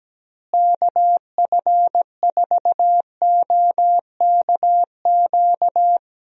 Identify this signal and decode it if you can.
Morse code, heard as a keyed tone, about 17 wpm, 700 Hz. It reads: KF4OKQ
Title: USA Call Signs - 6 Characters - Repeated 3x - 17wpm